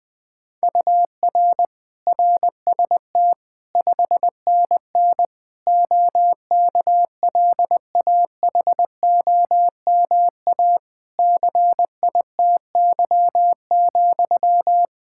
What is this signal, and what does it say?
Morse code, heard as a keyed tone, about 20 wpm, 700 Hz. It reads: UR RST 5NN OKLAHOMA CITY,
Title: Practice QSO - Exchange 1 - Beginning - 20wpm